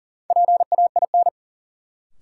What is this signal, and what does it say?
Morse code, heard as a keyed tone, about 40 wpm, 700 Hz. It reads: PAIN
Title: Sentences from Top 1600 Words - New Word Drill Alt - 40wpm